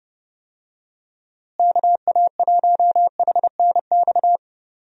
Morse code, heard as a keyed tone, about 30 wpm, 700 Hz. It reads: KA1HNX
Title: USA Call Signs - 6 Characters - Repeated 2x - 30wpm